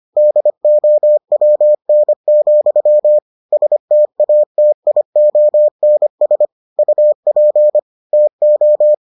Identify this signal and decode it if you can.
Morse code, heard as a keyed tone, about 25 wpm, 600 Hz. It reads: DOWN, STATIONS UP TO